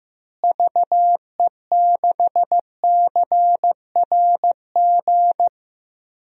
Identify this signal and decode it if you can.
Morse code, heard as a keyed tone, about 15 wpm, 700 Hz. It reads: VE6CRG
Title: DXCC-100 Call Signs - 6 Characters - Repeated 3x - 15wpm